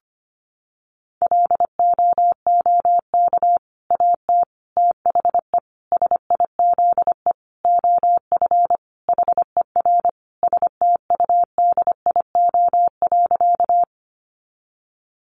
Morse code, heard as a keyed tone, about 25 wpm, 700 Hz. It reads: LOOK AT THE SIZE OF HER STUDIO.
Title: Sentences from Top 1200 Words 25wpm